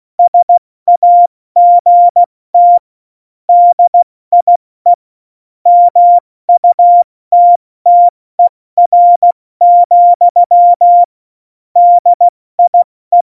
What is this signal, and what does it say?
Morse code, heard as a keyed tone, about 16 wpm, 700 Hz. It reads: SAGT DIE MUTTER, DIE
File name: Morsekurs / Witze / Witz 22. August 17, Tempo 70 F80